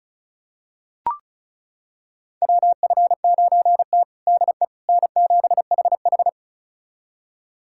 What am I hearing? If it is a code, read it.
Morse code, heard as a keyed tone, about 34 wpm, 700 Hz. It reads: WF9T DE N7HH
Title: USA Callsign DE Callsign - 4 Characters - 35wpm